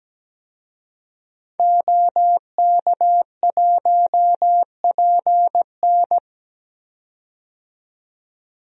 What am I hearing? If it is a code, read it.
Morse code, heard as a keyed tone, about 17 wpm, 700 Hz. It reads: OK1PN
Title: DXCC-100 Call Signs - 5 Characters - 17wpm